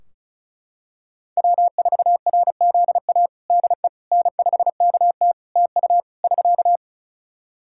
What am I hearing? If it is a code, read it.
Morse code, heard as a keyed tone, about 36 wpm, 700 Hz. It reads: W4RZA DE N5KT TU <SK>
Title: Practice QSO - Exchange 4 - Ending - 35wpm